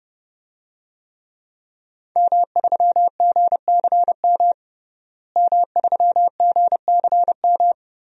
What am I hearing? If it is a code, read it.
Morse code, heard as a keyed tone, about 32 wpm, 700 Hz. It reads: M3GCM M3GCM
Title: DXCC-100 Call Signs - 5 Characters - Repeated 2x - 30wpm